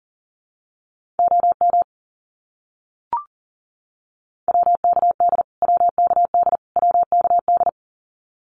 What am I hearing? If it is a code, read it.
Morse code, heard as a keyed tone, about 37 wpm, 700 Hz. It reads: OM WKD WKD WKD
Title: Abbreviations Repeated 3x - 40wpm